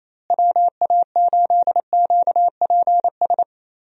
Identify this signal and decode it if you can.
Morse code, heard as a keyed tone, about 28 wpm, 700 Hz. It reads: WA8QPS
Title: USA Call Signs - 6 Characters - Repeated 3x - 28wpm